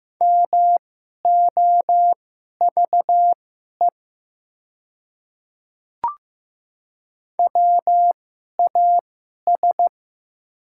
Morse code, heard as a keyed tone, about 15 wpm, spaced as for 11 wpm, 700 Hz. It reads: MOVE WAS
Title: 2x Character Spacing - Top 200 Words - 15wpm